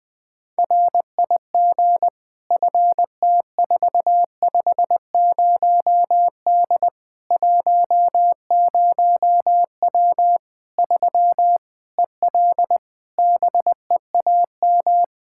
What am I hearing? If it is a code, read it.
Morse code, heard as a keyed tone, about 20 wpm, 700 Hz. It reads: RIG FT450D 10W 3 EL BEAM
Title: Practice QSO - Exchange 2 - Info - 20wpm